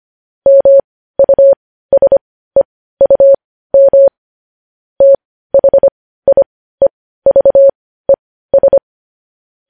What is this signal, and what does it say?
Morse code, heard as a keyed tone, about 25 wpm, spaced as for 15 wpm, 550 Hz. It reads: MUSEUM THIEVES
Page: news-headlines-in-morse-code-at-15-wpm